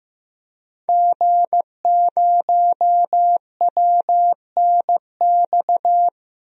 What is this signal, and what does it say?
Morse code, heard as a keyed tone, about 15 wpm, 700 Hz. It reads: G0WNX
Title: DXCC-100 Call Signs - 5 Characters - Repeated 3x - 15wpm